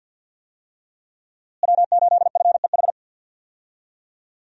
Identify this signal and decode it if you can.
Morse code, heard as a keyed tone, about 52 wpm, 700 Hz. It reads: W8FEH
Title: USA Call Signs - 5 Characters - Repeated 2x - 50wpm